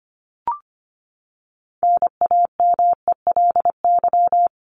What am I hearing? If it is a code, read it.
Morse code, heard as a keyed tone, about 25 wpm, 700 Hz. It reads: NAMELY
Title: LY Suffix - Top 100 Words with Base Words 25wpm